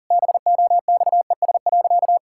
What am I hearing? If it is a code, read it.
Morse code, heard as a keyed tone, about 40 wpm, 700 Hz. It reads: BOXES.
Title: Pangram Challenge v2 - 40wpm